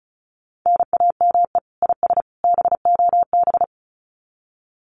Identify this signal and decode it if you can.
Morse code, heard as a keyed tone, about 34 wpm, 700 Hz. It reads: NAME IS BOB
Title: Common QSO Phrases 35wpm